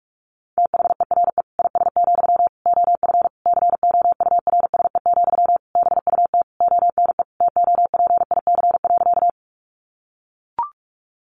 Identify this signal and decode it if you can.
Morse code, heard as a keyed tone, about 44 wpm, 700 Hz. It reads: THERE IS, OF COURSE, BUT ONE TOPIC.
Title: Sentences from Top 1200 Words 45wpm